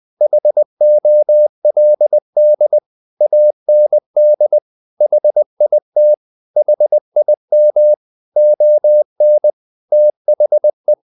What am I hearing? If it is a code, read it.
Morse code, heard as a keyed tone, about 20 wpm, 600 Hz. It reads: HOLD AND HIT HIM ON THE
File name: the old man and the sea part 2 20wpm